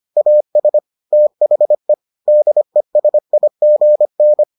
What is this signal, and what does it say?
Morse code, heard as a keyed tone, about 25 wpm, 600 Hz. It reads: AS THE DESIGN